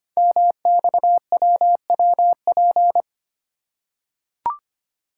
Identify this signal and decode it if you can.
Morse code, heard as a keyed tone, about 25 wpm, 700 Hz. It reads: MXWWP
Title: Code Groups - 5 Letters - 25wpm